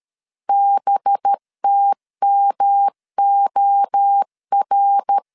Morse code, heard as a keyed tone, about 13 wpm, 800 Hz. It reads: BTMOR
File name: CW / DET / 1 6 1500C S A 12wpm 3440